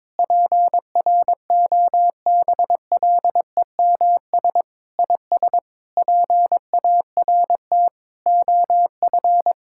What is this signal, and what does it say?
Morse code, heard as a keyed tone, about 22 wpm, 700 Hz. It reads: PROBLEMS IS PART OF
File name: practice / 22wpm / Sentences with 3-Word Phrases - 22wpm